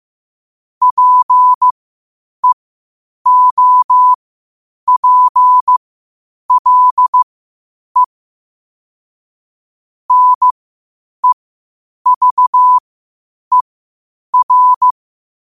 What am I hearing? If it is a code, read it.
Morse code, heard as a keyed tone, about 15 wpm, spaced as for 8 wpm, 1000 Hz. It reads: PEOPLE NEVER
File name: qotdfiles / 8wpm